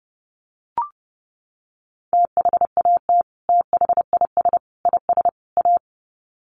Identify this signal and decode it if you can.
Morse code, heard as a keyed tone, about 30 wpm, 700 Hz. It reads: THAT THIS IS A
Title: Sets of 4 Words 30wpm